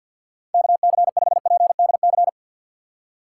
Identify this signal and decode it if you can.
Morse code, heard as a keyed tone, about 52 wpm, 700 Hz. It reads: KC5PDC
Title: USA Call Signs - 4x Character Spacing - 50wpm